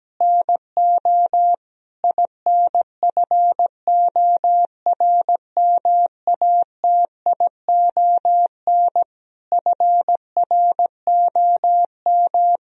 Morse code, heard as a keyed tone, about 17 wpm, 700 Hz. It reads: NO INFORMATION FROM
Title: Sentences from Top 1000 Words - Review Part 3 - 17wpm